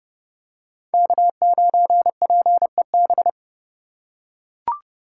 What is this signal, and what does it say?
Morse code, heard as a keyed tone, about 30 wpm, 700 Hz. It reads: K9PEB
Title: USA Call Signs - 5 Characters - 4x Character Spacing - 30wpm